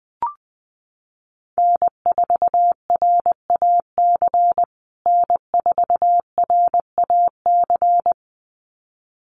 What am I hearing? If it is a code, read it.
Morse code, heard as a keyed tone, about 20 wpm, 700 Hz. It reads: N4RAC N4RAC
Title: USA Call Signs Repeated 2x 20wpm